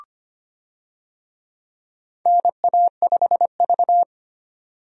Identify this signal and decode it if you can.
Morse code, heard as a keyed tone, about 25 wpm, 700 Hz. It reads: NA5V
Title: USA Call Signs - 4 Characters - 25wpm